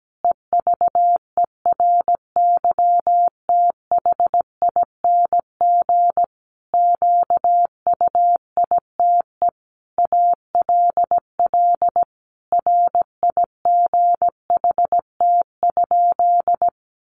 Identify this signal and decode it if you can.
Morse code, heard as a keyed tone, about 17 wpm, 700 Hz. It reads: EVERYTHING QUITE ALL RIGHT?
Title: Sentences from Top 700 Words 17wpm